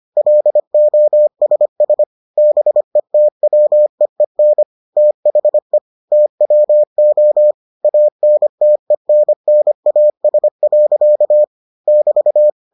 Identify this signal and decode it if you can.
Morse code, heard as a keyed tone, about 25 wpm, 600 Hz. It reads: LOSS BETWEEN THE TWO ANTENNAS. <BT>